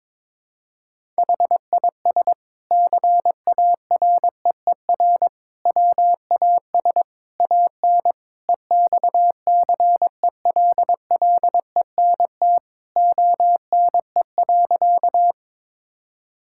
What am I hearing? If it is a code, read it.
Morse code, heard as a keyed tone, about 22 wpm, 700 Hz. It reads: HIS CAREER WAS AN EXCELLENT ONE.
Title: Sentences from Top 1000 Words - Review Part 6 - 22wpm